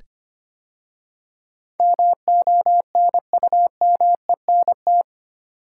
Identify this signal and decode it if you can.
Morse code, heard as a keyed tone, about 25 wpm, 700 Hz. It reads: MONUMENT
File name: 8 Letter Words - First 3 Letters Given - 25wpm